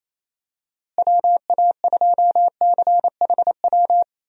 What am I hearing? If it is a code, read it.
Morse code, heard as a keyed tone, about 28 wpm, 700 Hz. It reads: WA2CHW
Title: USA Call Signs - 6 Characters - Repeated 3x - 28wpm